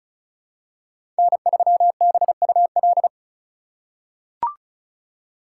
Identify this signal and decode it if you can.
Morse code, heard as a keyed tone, about 34 wpm, 700 Hz. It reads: N3BUL
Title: USA Call Signs - 5 Characters - 2x Character Spacing - 35wpm